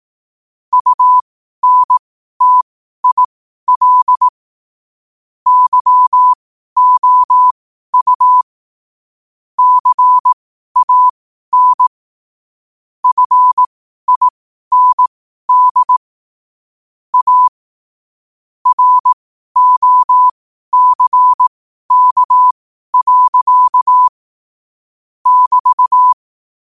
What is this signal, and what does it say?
Morse code, heard as a keyed tone, about 18 wpm, spaced as for 12 wpm, 1000 Hz. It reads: UNTIL YOU CAN FIND A ROCK. <BT>
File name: qotdfiles / 12wpm